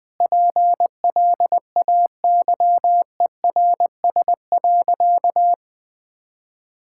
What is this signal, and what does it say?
Morse code, heard as a keyed tone, about 20 wpm, 700 Hz. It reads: PLAYERS.
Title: Sentences with 5-Word Phrases - 20wpm